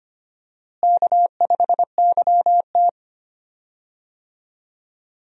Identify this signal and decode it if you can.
Morse code, heard as a keyed tone, about 25 wpm, 700 Hz. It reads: K5YT
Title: USA Call Signs - 4 Characters - Repeated 2x - 25wpm